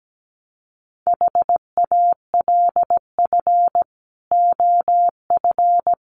Morse code, heard as a keyed tone, about 17 wpm, 700 Hz. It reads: HALF OF
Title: Sets of 3 Words 17wpm